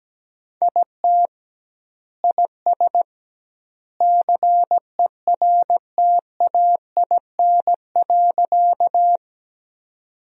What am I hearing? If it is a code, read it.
Morse code, heard as a keyed tone, about 17 wpm, 700 Hz. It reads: IT IS CERTAIN.
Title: 2x Word-Spacing - Sentences from Top 500 Words - 17wpm